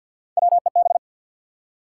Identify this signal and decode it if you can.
Morse code, heard as a keyed tone, about 51 wpm, 700 Hz. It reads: WEB
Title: Three Letter Words - 50wpm